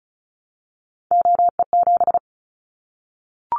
Morse code, heard as a keyed tone, about 36 wpm, 700 Hz. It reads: OE7
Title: Callsign Prefixes for Top 50 DXCC - 35wpm